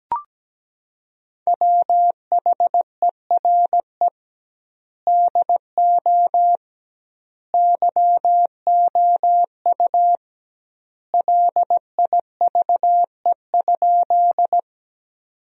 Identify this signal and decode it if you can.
Morse code, heard as a keyed tone, about 17 wpm, 700 Hz. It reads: WHERE DO YOU LIVE?
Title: 2x Word-Spacing - Sentences from Top 200 Words - 17wpm